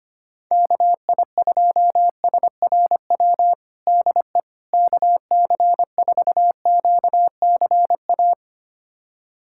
Morse code, heard as a keyed tone, about 25 wpm, 700 Hz. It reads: KI2SRW DE KC4QCA
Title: USA Callsign DE Callsign 25wpm